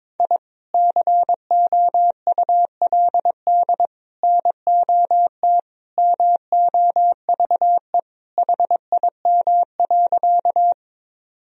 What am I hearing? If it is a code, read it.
Morse code, heard as a keyed tone, about 22 wpm, 700 Hz. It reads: I COULD NOT MOVE HIM.
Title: Sentences from Top 1000 Words - Review Part 1 - 22wpm